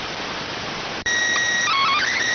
SSTV (slow-scan television), PD 290.